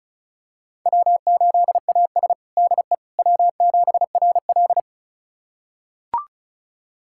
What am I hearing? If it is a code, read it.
Morse code, heard as a keyed tone, about 36 wpm, 700 Hz. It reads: W8AS DE W7RL
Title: USA Callsign DE Callsign - 4 Characters - 35wpm